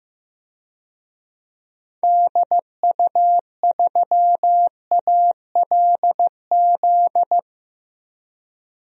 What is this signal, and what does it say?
Morse code, heard as a keyed tone, about 15 wpm, 700 Hz. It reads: DU3ALZ
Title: DXCC-100 Call Signs - 6 Characters - 15wpm